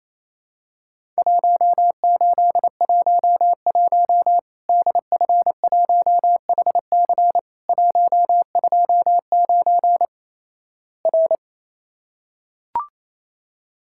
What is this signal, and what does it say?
Morse code, heard as a keyed tone, about 28 wpm, 700 Hz. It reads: 1811 DF1HC 129 R
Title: Worked All Europe DX Contest - QTC Exchange - 28wpm